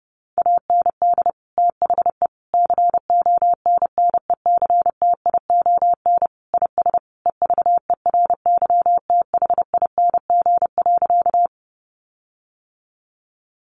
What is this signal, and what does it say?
Morse code, heard as a keyed tone, about 30 wpm, 700 Hz. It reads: AND THE CONNECTION IS EVERYTHING.
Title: Sentences from Top 1300 Words 30wpm